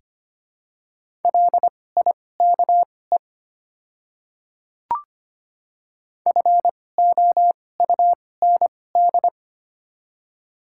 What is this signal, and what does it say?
Morse code, heard as a keyed tone, about 25 wpm, spaced as for 18 wpm, 700 Hz. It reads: LIKE FOUND
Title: 2x Character Spacing - Top 300 Words - 25wpm